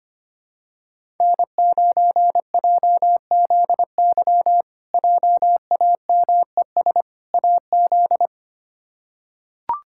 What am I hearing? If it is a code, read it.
Morse code, heard as a keyed tone, about 25 wpm, 700 Hz. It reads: N9JZY JAMES AZ
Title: K1USN SST QSO Practice - 25wpm